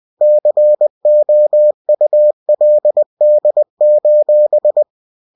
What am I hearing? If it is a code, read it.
Morse code, heard as a keyed tone, about 20 wpm, 600 Hz. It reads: COULD:
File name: cw-books / winnie-the-pooh / winnie-20wpm-page-0071